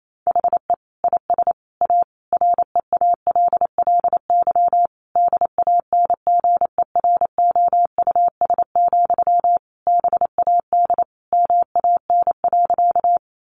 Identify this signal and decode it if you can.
Morse code, heard as a keyed tone, about 28 wpm, 700 Hz. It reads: HE IS A REALLY DANGEROUS, BAD MAN.
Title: Sentences from Top 1200 Words 28wpm